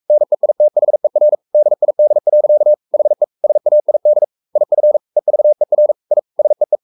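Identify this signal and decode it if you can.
Morse code, heard as a keyed tone, about 44 wpm, 600 Hz. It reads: NEITHER DID. HE SAID IF EVER I SEE
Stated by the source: Morse code at 43 wpm sending Ken Follett's novel Never in English, from CW med Gnister.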